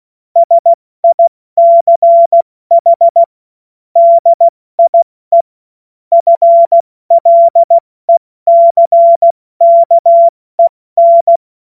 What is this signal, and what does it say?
Morse code, heard as a keyed tone, about 16 wpm, 680 Hz. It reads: SICH DIE FLECKEN
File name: Morsekurs / Witze / Witz 20 Februar 2024 70 F80